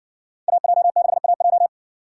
Morse code, heard as a keyed tone, about 60 wpm, 700 Hz. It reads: AJ6AJ